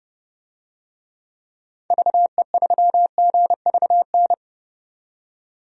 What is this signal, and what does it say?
Morse code, heard as a keyed tone, about 30 wpm, 700 Hz. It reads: VE3GVN
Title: DXCC-100 Call Signs - 6 Characters - 30wpm